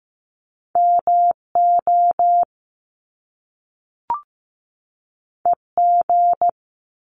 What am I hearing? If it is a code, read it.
Morse code, heard as a keyed tone, about 15 wpm, 700 Hz. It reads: MO EG
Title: Top Two Letter Combinations in Words 15wpm